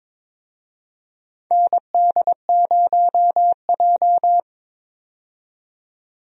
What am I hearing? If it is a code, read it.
Morse code, heard as a keyed tone, about 22 wpm, 700 Hz. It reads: ND0J
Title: USA Call Signs - 4 Characters - 22wpm